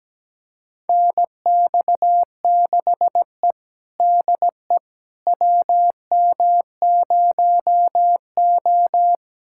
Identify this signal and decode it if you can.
Morse code, heard as a keyed tone, about 17 wpm, 700 Hz. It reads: NX6E DE WM0O
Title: USA Callsign DE Callsign - 4 Characters - 17wpm